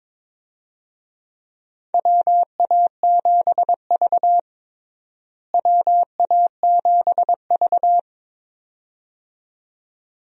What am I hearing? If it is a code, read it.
Morse code, heard as a keyed tone, about 22 wpm, 700 Hz. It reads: WA7V WA7V
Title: USA Call Signs - 4 Characters - Repeated 2x - 22wpm